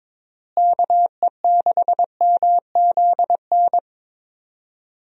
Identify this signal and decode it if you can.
Morse code, heard as a keyed tone, about 22 wpm, 700 Hz. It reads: KE6MZN
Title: USA Call Signs - 6 Characters - Repeated 3x - 22wpm